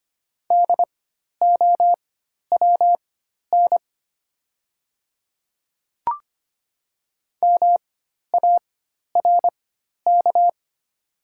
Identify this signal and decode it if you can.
Morse code, heard as a keyed tone, about 25 wpm, spaced as for 12 wpm, 690 Hz. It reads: DOWN MARK
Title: 4x Character Spacing - Top 300 Words - 25wpm